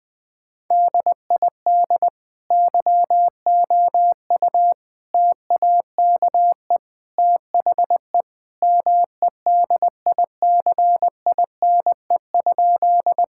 Morse code, heard as a keyed tone, about 21 wpm, 700 Hz. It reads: DID YOU TAKE THE MEDICINE?
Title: Sentences from Top 1000 Words - Review Part 6 - 20wpm